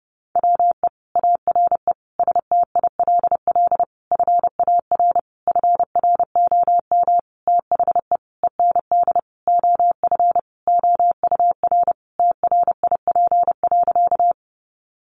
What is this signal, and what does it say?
Morse code, heard as a keyed tone, about 30 wpm, 700 Hz. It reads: WE ARE STILL FAR FROM THE END OF OUR TRIP.
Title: Sentences from Top 1300 Words 30wpm